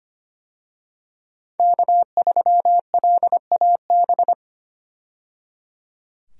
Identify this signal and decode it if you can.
Morse code, heard as a keyed tone, about 25 wpm, 700 Hz. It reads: K3LAB
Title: USA Call Signs - 5 Characters - 25wpm